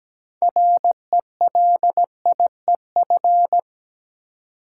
Morse code, heard as a keyed tone, about 17 wpm, 700 Hz. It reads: RELIEF
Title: Six Letter Words - 17wpm